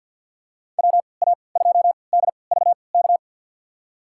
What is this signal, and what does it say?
Morse code, heard as a keyed tone, about 53 wpm, spaced as for 29 wpm, 700 Hz. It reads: WA2DVK
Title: USA Call Signs - 6 Characters - 3x Character Spacing - 50wpm